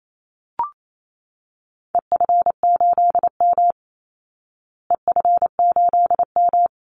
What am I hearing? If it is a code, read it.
Morse code, heard as a keyed tone, about 28 wpm, 700 Hz. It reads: EF8M EF8M
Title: Call Signs Repeated 2x - 28wpm